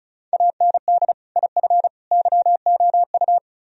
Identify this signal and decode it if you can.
Morse code, heard as a keyed tone, about 44 wpm, 700 Hz. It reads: AND IF YOU